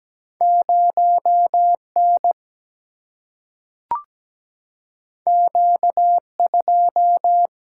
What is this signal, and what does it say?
Morse code, heard as a keyed tone, about 17 wpm, 700 Hz. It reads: 0N Q2
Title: Single-Letter Single-Number 17wpm